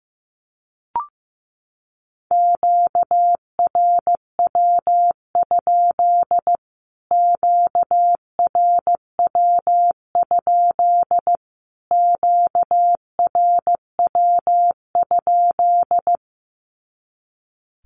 Morse code, heard as a keyed tone, about 15 wpm, 700 Hz. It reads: QRW? QRW? QRW?
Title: Q-Codes Repeated 3x 15wpm